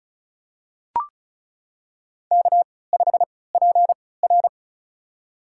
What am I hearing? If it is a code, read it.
Morse code, heard as a keyed tone, about 36 wpm, spaced as for 20 wpm, 700 Hz. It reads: K5PR
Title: USA Call Signs - 4 Characters - 3x Character Spacing - 35wpm